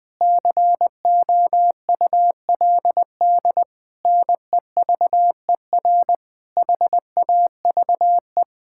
Morse code, heard as a keyed tone, about 20 wpm, 700 Hz. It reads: COULD NEVER HAVE
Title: Sentences from Top 1000 Words - Review Part 6 - 20wpm